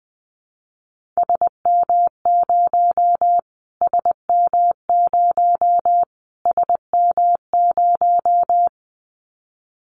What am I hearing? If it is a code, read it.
Morse code, heard as a keyed tone, about 20 wpm, 700 Hz. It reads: SM0 SM0 SM0
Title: Callsign Prefixes for Top 25 DXCC - Repeated 3x - 20wpm